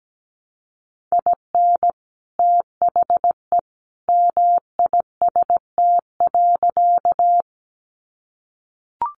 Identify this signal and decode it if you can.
Morse code, heard as a keyed tone, about 17 wpm, 700 Hz. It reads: IN THE MIST.
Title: A Sound Of Thunder 17wpm